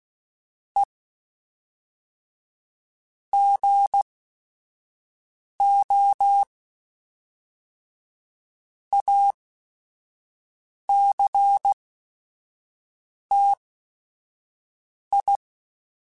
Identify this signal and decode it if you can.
Morse code, heard as a keyed tone, about 16 wpm, spaced as for 5 wpm, 800 Hz. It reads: E GO ACTI
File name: Technical Subjects / Technical Subjects-mp3s / OR Gate @5